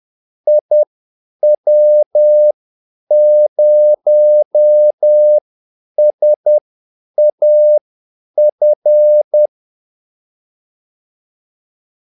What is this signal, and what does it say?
Morse code, heard as a keyed tone, about 10 wpm, spaced as for 8 wpm, 600 Hz. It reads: IW0SAF
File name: Callsigns-10-8wpm